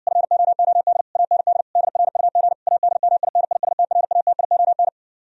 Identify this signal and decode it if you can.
Morse code, heard as a keyed tone, about 59 wpm, 700 Hz. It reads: FOOD AND DRUG ADMINISTRATION